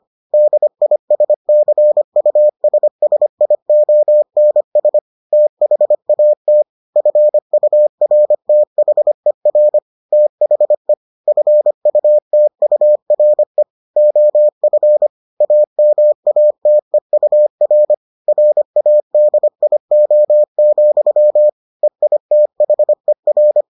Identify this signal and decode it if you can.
Morse code, heard as a keyed tone, about 25 wpm, 600 Hz. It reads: DISCUSSIONS THAT FURTHER THE FUTURE OF AMATEUR RADIO, EITHER